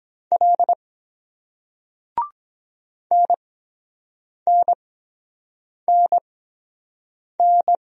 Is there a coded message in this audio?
Yes: Morse code from 0.3 to 7.8 s, heard as a keyed tone, about 23 wpm, spaced as for 7 wpm, 700 Hz. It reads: L NNNN